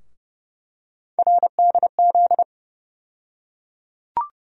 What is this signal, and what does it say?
Morse code, heard as a keyed tone, about 30 wpm, 700 Hz. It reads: RDZ
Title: USA Call Sign Postfixes - 3 Letters 30wpm